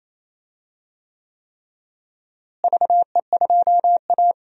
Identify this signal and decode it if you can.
Morse code, heard as a keyed tone, about 28 wpm, 700 Hz. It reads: VE2A
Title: DXCC-100 Call Signs - 4 Characters - Repeated 3x - 28wpm